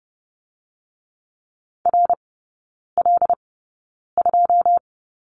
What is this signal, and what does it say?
Morse code, heard as a keyed tone, about 30 wpm, spaced as for 9 wpm, 700 Hz. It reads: RL2
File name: Lesson 18e - Letter m - Three Characters 30wpm